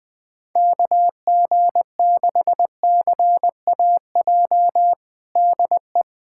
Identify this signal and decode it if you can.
Morse code, heard as a keyed tone, about 20 wpm, 700 Hz. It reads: KG6CAJ DE
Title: USA Callsign DE Callsign - 6 Characters - 20wpm